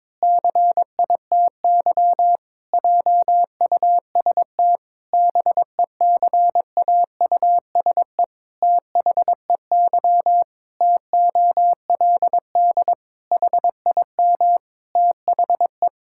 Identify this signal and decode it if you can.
Morse code, heard as a keyed tone, about 22 wpm, 700 Hz. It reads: CITY JUST BECAUSE THEY TOLD HIM THE
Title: Pangram Challenge v2 - 22wpm